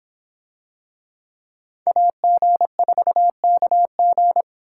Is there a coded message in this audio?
Yes: Morse code from 1.9 to 4.4 s, heard as a keyed tone, about 26 wpm, 700 Hz. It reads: AG4KG